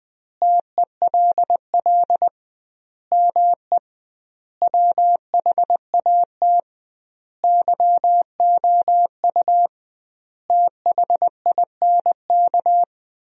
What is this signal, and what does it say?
Morse code, heard as a keyed tone, about 20 wpm, 700 Hz. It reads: TELL ME WHAT YOU THINK
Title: 2x Word-Spacing - Sentences from Top 200 Words - 20wpm